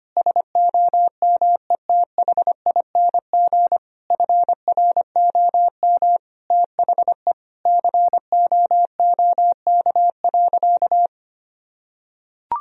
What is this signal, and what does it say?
Morse code, heard as a keyed tone, about 25 wpm, 700 Hz. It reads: SOMETHING FROM THE COOK.
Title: Sentences from Top 1000 Words - Review Part 4 - 25wpm